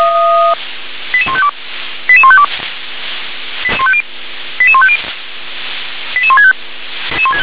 Five-tone selective call (ZVEI), heard as quick runs of five tones, tones of about 70 ms. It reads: 88152 88152 88150 88165